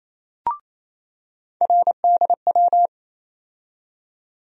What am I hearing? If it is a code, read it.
Morse code, heard as a keyed tone, about 28 wpm, 700 Hz. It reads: RDW
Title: USA Call Sign Postfixes - 3 Letters 28wpm